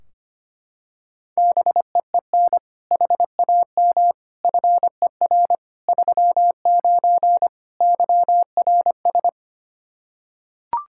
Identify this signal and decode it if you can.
Morse code, heard as a keyed tone, about 25 wpm, 700 Hz. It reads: BEEN HAM FER 39 YRS
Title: Practice QSO - Exchange 3 - More Info - 25wpm